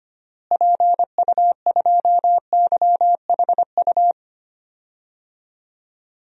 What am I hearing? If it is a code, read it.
Morse code, heard as a keyed tone, about 25 wpm, 700 Hz. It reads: PU2YHU
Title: DXCC-100 Call Signs - 6 Characters - Repeated 2x - 25wpm